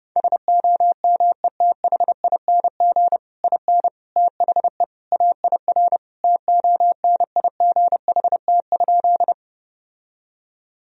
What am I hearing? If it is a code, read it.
Morse code, heard as a keyed tone, about 30 wpm, 700 Hz. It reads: SOMETHING IN THE AIR TONIGHT?
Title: Sentences from Top 1000 Words - Review Part 1 - 30wpm